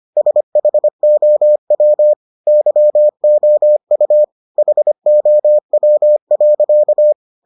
Morse code, heard as a keyed tone, about 25 wpm, 600 Hz. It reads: SHOW YOU HOW.